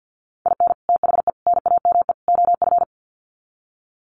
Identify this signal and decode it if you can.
Morse code, heard as a keyed tone, about 50 wpm, 700 Hz. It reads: IN THE NAME OF
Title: Sets of 4 Words 50wpm